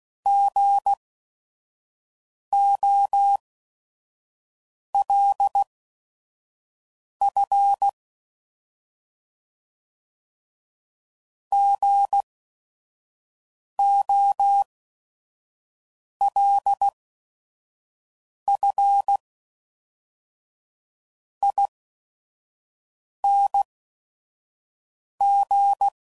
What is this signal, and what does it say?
Morse code, heard as a keyed tone, about 16 wpm, spaced as for 5 wpm, 800 Hz. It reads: GOLF GOLFING
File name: Prefixes / By First Letter / Gs